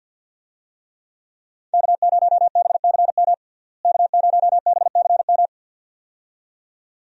Morse code, heard as a keyed tone, about 50 wpm, 700 Hz. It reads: K0BCM K0BCM
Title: USA Call Signs - 5 Characters - Repeated 2x - 50wpm